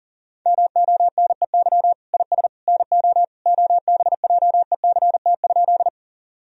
Morse code, heard as a keyed tone, about 40 wpm, 700 Hz. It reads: MONEY IS NO OBJECT?
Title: Sentences from Top 1000 Words - Review Part 3 - 40wpm